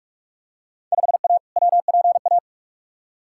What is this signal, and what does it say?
Morse code, heard as a keyed tone, about 46 wpm, 700 Hz. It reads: 5A WPA
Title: Field Day Exchanges Encore - 45wpm